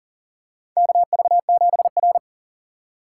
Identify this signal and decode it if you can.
Morse code, heard as a keyed tone, about 42 wpm, 700 Hz. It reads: KV7R
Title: USA Call Signs - 4 Characters - 4x Character Spacing - 40wpm